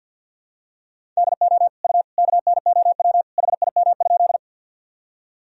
Morse code, heard as a keyed tone, about 50 wpm, 700 Hz. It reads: DO U KNOW HIM?
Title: Sentences from Top 100 Words and Abbreviations 50wpm